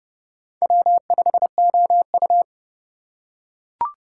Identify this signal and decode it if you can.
Morse code, heard as a keyed tone, about 30 wpm, 700 Hz. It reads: W5OU
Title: USA Call Signs - 4 Characters - 4x Character Spacing - 30wpm